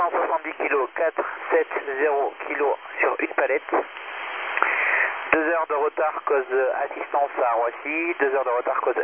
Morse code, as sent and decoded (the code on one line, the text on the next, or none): none